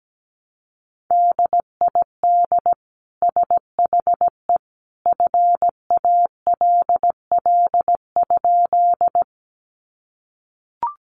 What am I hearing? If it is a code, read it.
Morse code, heard as a keyed tone, about 17 wpm, 700 Hz. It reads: DID SHE FALL?
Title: Sentences from Top 500 Words 17wpm